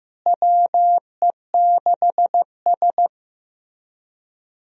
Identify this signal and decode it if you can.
Morse code, heard as a keyed tone, about 15 wpm, 700 Hz. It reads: WE6S
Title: USA Call Signs - 4 Characters - Repeated 3x - 15wpm